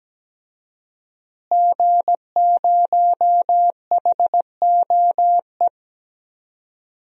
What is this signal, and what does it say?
Morse code, heard as a keyed tone, about 17 wpm, 700 Hz. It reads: G0HOE
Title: DXCC-100 Call Signs - 5 Characters - Repeated 3x - 17wpm